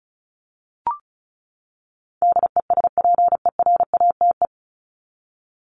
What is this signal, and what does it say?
Morse code, heard as a keyed tone, about 34 wpm, 700 Hz. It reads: DESPERATE
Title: Top 1751-2000 Words 35wpm